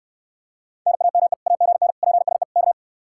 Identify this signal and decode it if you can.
Morse code, heard as a keyed tone, about 69 wpm, 700 Hz. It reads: NAME AGN PSE K